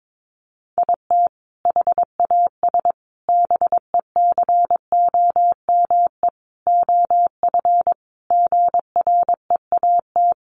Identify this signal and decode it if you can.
Morse code, heard as a keyed tone, about 22 wpm, 700 Hz. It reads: IT HAS BECOME OF GREAT
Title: Sentences from Top 1200 Words 22wpm